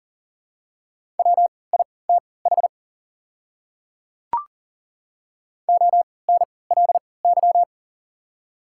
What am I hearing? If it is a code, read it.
Morse code, heard as a keyed tone, about 40 wpm, spaced as for 23 wpm, 700 Hz. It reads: WITH ONLY
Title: 3x Character Spacing - Top 200 Words - 40wpm